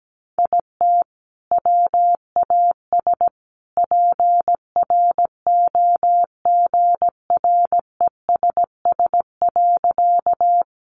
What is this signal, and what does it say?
Morse code, heard as a keyed tone, about 17 wpm, 700 Hz. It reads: IT WAS PROGRESS.
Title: Sentences from Top 1400 Words 17wpm